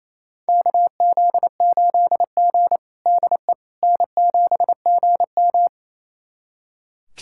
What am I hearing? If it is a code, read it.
Morse code, heard as a keyed tone, about 28 wpm, 700 Hz. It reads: KZ8G DE N7GM
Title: USA Callsign DE Callsign - 4 Characters - 28wpm